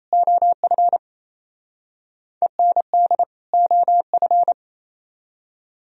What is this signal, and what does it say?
Morse code, heard as a keyed tone, about 35 wpm, 700 Hz. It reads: OF END OF